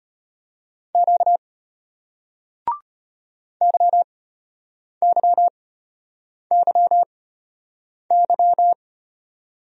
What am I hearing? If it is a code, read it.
Morse code, heard as a keyed tone, about 34 wpm, spaced as for 8 wpm, 700 Hz. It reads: Q YYYY